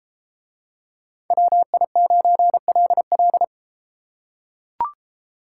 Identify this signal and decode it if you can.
Morse code, heard as a keyed tone, about 32 wpm, 700 Hz. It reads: WI9LL